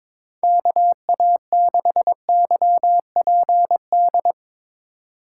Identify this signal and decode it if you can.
Morse code, heard as a keyed tone, about 22 wpm, 700 Hz. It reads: KA6YPD
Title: USA Call Signs - 6 Characters - 2x Character Spacing - 22wpm